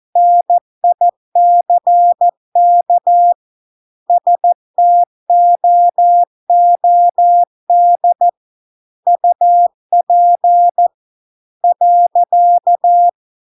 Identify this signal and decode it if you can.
Morse code, heard as a keyed tone, about 14 wpm, 700 Hz. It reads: NICK STOOD UP .